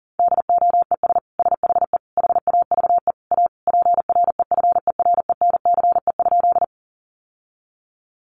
Morse code, heard as a keyed tone, about 40 wpm, 700 Hz. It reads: DOES SHE HAVE A PREFERENCE?
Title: Sentences from Top 1500 Words 40wpm